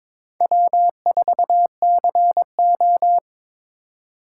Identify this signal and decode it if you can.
Morse code, heard as a keyed tone, about 26 wpm, 700 Hz. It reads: W4CO
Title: Speed Racing - USA 1x2 Call Signs 22wpm